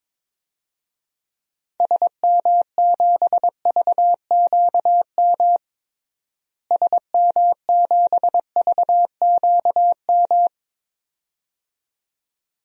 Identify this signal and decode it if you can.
Morse code, heard as a keyed tone, about 22 wpm, 700 Hz. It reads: SM7VQM SM7VQM
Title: DXCC-100 Call Signs - 6 Characters - Repeated 2x - 22wpm